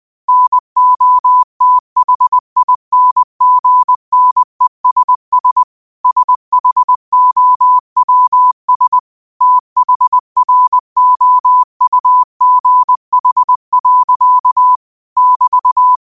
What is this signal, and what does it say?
Morse code, heard as a keyed tone, about 20 wpm, 1000 Hz. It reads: NOTHINGNESS SHOWS THROUGH. <BT>
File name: qotdfiles / 20wpm